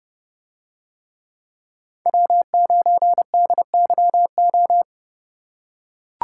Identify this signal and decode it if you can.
Morse code, heard as a keyed tone, about 30 wpm, 700 Hz. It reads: W9DYO